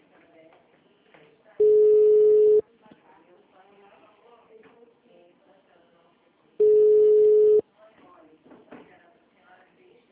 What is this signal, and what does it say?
Ringback tone (Europe, 425 Hz), with speech.
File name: out- -2001- - - .54705